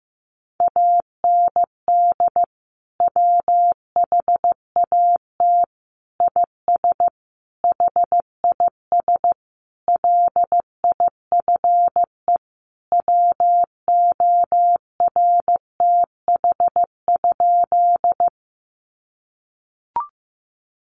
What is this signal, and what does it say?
Morse code, heard as a keyed tone, about 15 wpm, 700 Hz. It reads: AND WHAT IS HIS LIFE WORTH?
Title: Sentences from Top 800 Words 15wpm